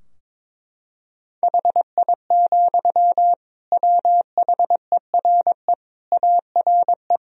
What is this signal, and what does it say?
Morse code, heard as a keyed tone, about 22 wpm, 700 Hz. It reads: HI, WHERE ARE
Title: Sentences from Top 1000 Words - Review Part 3 - 22wpm